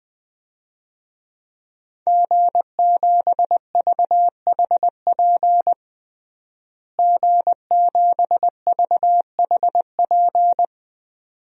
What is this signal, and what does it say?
Morse code, heard as a keyed tone, about 20 wpm, 700 Hz. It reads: G7VHP G7VHP
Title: DXCC-100 Call Signs - 5 Characters - Repeated 2x - 20wpm